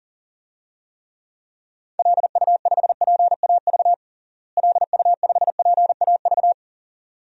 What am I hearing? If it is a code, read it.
Morse code, heard as a keyed tone, about 37 wpm, 700 Hz. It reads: LU5PAV LU5PAV
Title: DXCC-100 Call Signs - 6 Characters - Repeated 2x - 40wpm